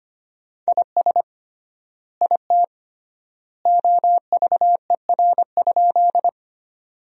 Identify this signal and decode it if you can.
Morse code, heard as a keyed tone, about 25 wpm, 700 Hz. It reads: IS IT OVER?
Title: 3x Word-Spacing - Sentences from Top 100 Words - 25wpm